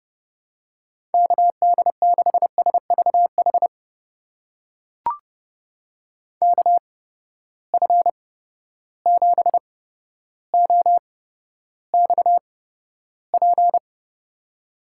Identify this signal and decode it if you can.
Morse code, heard as a keyed tone, about 30 wpm, spaced as for 13 wpm, 700 Hz. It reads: KD6SVH KF7OXP
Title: USA Call Signs - 6 Characters - 8x Character Spacing - 30wpm